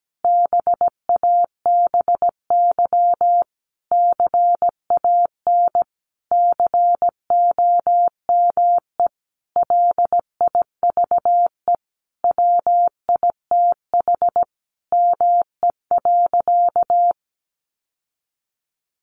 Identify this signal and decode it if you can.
Morse code, heard as a keyed tone, about 17 wpm, 700 Hz. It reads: BABY CAN COME LIVE WITH ME.
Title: Sentences from Top 1300 Words 17wpm